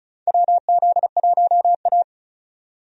Morse code, heard as a keyed tone, about 34 wpm, 700 Hz. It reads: WZ1A
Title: USA Call Signs - 4 Characters - 35wpm